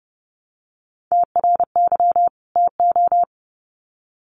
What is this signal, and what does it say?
Morse code, heard as a keyed tone, about 30 wpm, 700 Hz. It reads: TRY TO
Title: Sets of 2 Words 30wpm